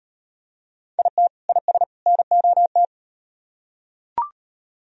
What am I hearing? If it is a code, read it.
Morse code, heard as a keyed tone, about 35 wpm, 700 Hz. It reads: IT IS NOT